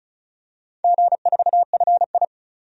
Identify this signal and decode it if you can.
Morse code, heard as a keyed tone, about 36 wpm, 700 Hz. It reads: G4FI
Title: DXCC-100 Call Signs - 4 Characters - 35wpm